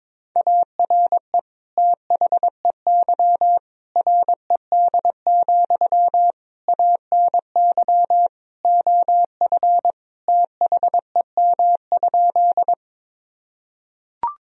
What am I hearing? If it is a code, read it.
Morse code, heard as a keyed tone, about 22 wpm, 700 Hz. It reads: ARE THEY RED, ANY OF THEM?
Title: Sentences from Top 1000 Words - Review Part 2 - 22wpm